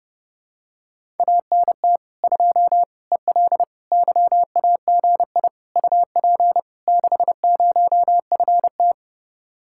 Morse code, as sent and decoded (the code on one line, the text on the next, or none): .- -. - / ..--- / . .-.. / -.-- .- --. .. / ..- .--. / -.... ----- ..-. -
ANT 2 EL YAGI UP 60FT